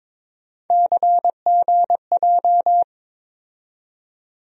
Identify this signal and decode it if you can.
Morse code, heard as a keyed tone, about 22 wpm, 700 Hz. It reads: CGJ
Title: USA Call Sign Postfixes - 3 Letters 22wpm